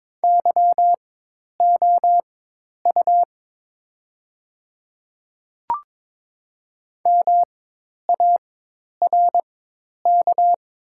Morse code, heard as a keyed tone, about 23 wpm, spaced as for 10 wpm, 700 Hz. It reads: YOU MARK
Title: 4x Character Spacing - Top 500 Words - 22wpm